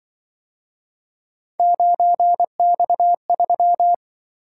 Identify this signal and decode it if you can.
Morse code, heard as a keyed tone, about 25 wpm, 700 Hz. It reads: 9X3
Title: Speed-Racing - Three Letter-Number Combinations - 20wpm